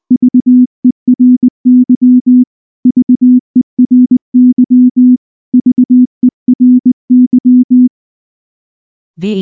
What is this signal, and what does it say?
Morse code, heard as a keyed tone, about 19 wpm, 250 Hz. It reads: VERY VERY VERY
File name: morse / morse-training-20WPM-7